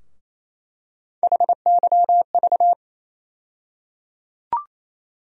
Morse code, heard as a keyed tone, about 28 wpm, 700 Hz. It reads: HYV
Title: USA Call Sign Postfixes - 3 Letters 28wpm